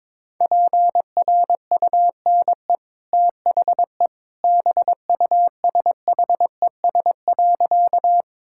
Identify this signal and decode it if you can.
Morse code, heard as a keyed tone, about 22 wpm, 690 Hz. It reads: PRUNE THE BUSHES.
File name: practice / 22wpm / Easy Sentences with Hints - 22wpm